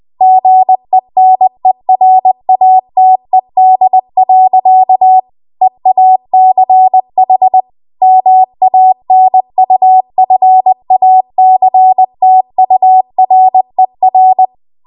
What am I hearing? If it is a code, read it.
Morse code, heard as a keyed tone, about 20 wpm, 750 Hz. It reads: GENERATED. EACH MANUFACTURER
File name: Archive / 20 WPM / 20WPM